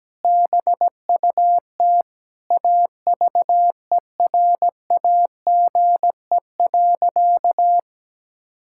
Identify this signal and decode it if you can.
Morse code, heard as a keyed tone, about 17 wpm, 700 Hz. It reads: BUT AVERAGE.
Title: Sentences from Top 1000 Words - Review Part 6 - 17wpm